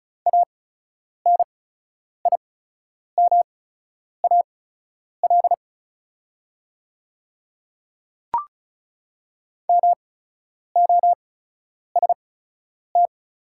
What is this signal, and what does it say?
Morse code, heard as a keyed tone, about 34 wpm, spaced as for 10 wpm, 690 Hz. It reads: ANIMAL MOST
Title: 8x Character Spacing - Top 200 Words - 35wpm